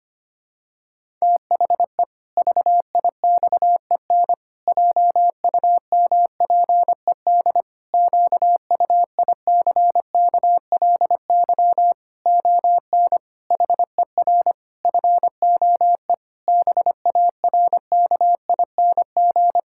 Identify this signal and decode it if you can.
Morse code, heard as a keyed tone, about 25 wpm, 700 Hz. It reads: THE VIXEN JUMPED QUICKLY ON HER FOE BARKING
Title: Pangram Challenge v2 - 25wpm